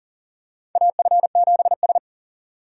Morse code, heard as a keyed tone, about 40 wpm, 700 Hz. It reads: AF7S
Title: USA Call Signs - 4 Characters - 3x Character Spacing - 40wpm